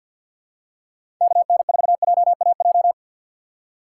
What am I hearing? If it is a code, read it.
Morse code, heard as a keyed tone, about 50 wpm, 700 Hz. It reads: KN4JAJ
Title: USA Call Signs - 6 Characters - 6x Character Spacing - 50wpm